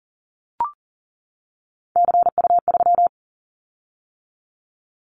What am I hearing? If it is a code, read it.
Morse code, heard as a keyed tone, about 41 wpm, 700 Hz. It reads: CU3
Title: Callsign Prefixes for Top 50 DXCC - 40wpm